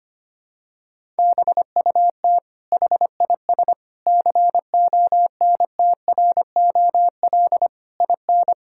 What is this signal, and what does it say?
Morse code, heard as a keyed tone, about 25 wpm, 700 Hz. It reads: BUT HIS CONTROL IN
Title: Sentences from Top 1000 Words - Review Part 6 - 25wpm